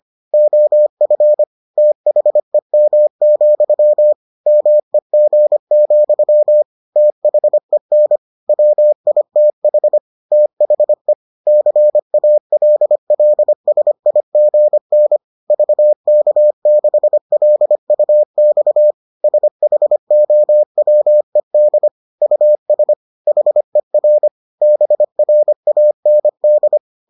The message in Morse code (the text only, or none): OF THEM, MEG, THEN WITH THE CALLSIGN VK6LUX SHOWED US HER BRAND